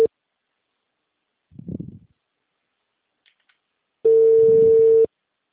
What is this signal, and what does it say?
Ringback tone (China, 450 Hz).